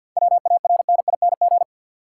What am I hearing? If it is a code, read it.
Morse code, heard as a keyed tone, about 50 wpm, 700 Hz. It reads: WARNING
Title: Seven Letter Words - 50wpm